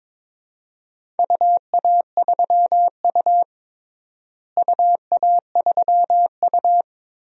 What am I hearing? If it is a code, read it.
Morse code, heard as a keyed tone, about 22 wpm, 700 Hz. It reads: UA3U UA3U
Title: DXCC-100 Call Signs - 4 Characters - Repeated 2x - 22wpm